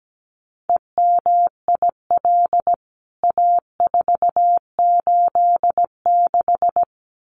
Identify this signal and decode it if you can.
Morse code, heard as a keyed tone, about 17 wpm, 700 Hz. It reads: EMIL A486
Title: CWT Exchange 17wpm